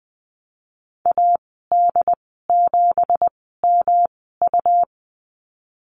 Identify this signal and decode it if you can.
Morse code, heard as a keyed tone, about 20 wpm, spaced as for 14 wpm, 700 Hz. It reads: AD7MU
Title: 193 Lesson 31g - Letter v - Call Signs 20wpm